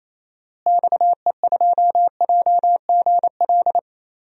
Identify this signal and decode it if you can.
Morse code, heard as a keyed tone, about 28 wpm, 700 Hz. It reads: XE2JGL
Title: DXCC-100 Call Signs - 6 Characters - Repeated 3x - 28wpm